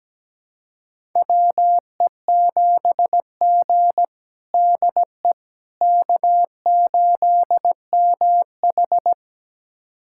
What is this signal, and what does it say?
Morse code, heard as a keyed tone, about 17 wpm, 700 Hz. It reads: WE7G DE K8MH
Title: USA Callsign DE Callsign - 4 Characters - 17wpm